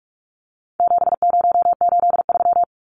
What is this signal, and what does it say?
Morse code, heard as a keyed tone, about 44 wpm, 700 Hz. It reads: 7083
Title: Numbers - 4-Digit 45wpm